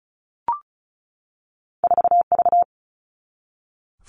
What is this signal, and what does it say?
Morse code, heard as a keyed tone, about 36 wpm, 700 Hz. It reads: 4V
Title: All Cut Numbers - 2-Digit 35wpm